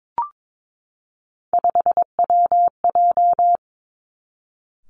Morse code, heard as a keyed tone, about 22 wpm, 700 Hz. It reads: 5WJ
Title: Three Challenging Letter-Number Combinations 22wpm